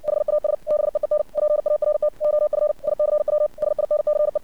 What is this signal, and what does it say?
Morse code, heard as a keyed tone, about 60 wpm, 600 Hz. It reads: HAS BEEN PART OF ICW SINCE
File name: archived / CRSnet